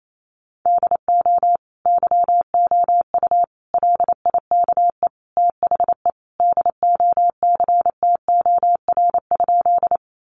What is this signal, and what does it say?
Morse code, heard as a keyed tone, about 28 wpm, 700 Hz. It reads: DO YOU LIKE THE DOCTOR?
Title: Sentences from Top 1500 Words 28wpm